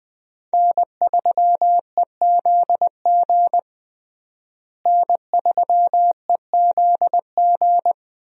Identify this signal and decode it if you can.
Morse code, heard as a keyed tone, about 20 wpm, 700 Hz. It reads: N3EZG N3EZG
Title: USA Call Signs - 5 Characters - Repeated 3x - 20wpm